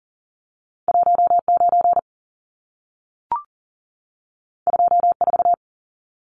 Morse code, heard as a keyed tone, about 40 wpm, 700 Hz. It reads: 19 24
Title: Numbers - 2-Digit 40wpm